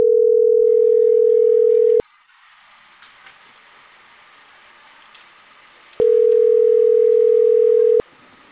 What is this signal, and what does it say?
Ringback tone (North America), 440 + 480 Hz.